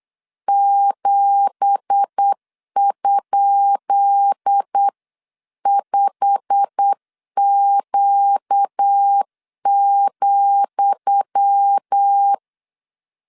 Morse code, heard as a keyed tone, about 8 wpm, 800 Hz. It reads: 7?5Q,
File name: LEZIONI / 10 SIGNS / 10D COMA 8wpm 3749